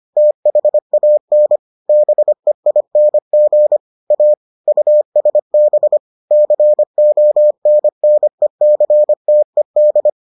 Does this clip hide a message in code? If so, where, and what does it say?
Yes: Morse code from 0.2 to 10.1 s, heard as a keyed tone, about 25 wpm, 600 Hz. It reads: THAN BEING A USB CONNECTED